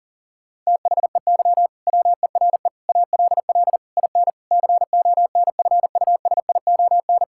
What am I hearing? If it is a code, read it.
Morse code, heard as a keyed tone, about 40 wpm, 700 Hz. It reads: THEY WERE ALL IN CONFUSION
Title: Sentences from Top 1600 Words 40wpm